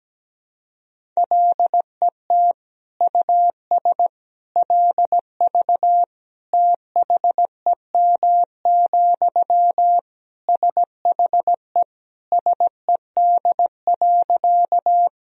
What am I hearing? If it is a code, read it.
Morse code, heard as a keyed tone, about 17 wpm, 700 Hz. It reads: LET US LV THEM, SHE SED.
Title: Sentences from Top 400 Words and Abbreviations 17wpm